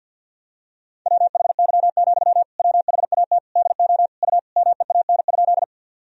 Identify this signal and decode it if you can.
Morse code, heard as a keyed tone, about 51 wpm, 700 Hz. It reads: WHY, WHAT DO U MEAN?
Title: Sentences from Top 200 Words and Abbreviations 50wpm